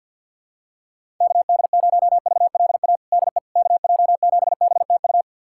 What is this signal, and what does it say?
Morse code, heard as a keyed tone, about 52 wpm, 700 Hz. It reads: KD0VLA DE KJ7BTU
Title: USA Callsign DE Callsign - 6 Characters - 50wpm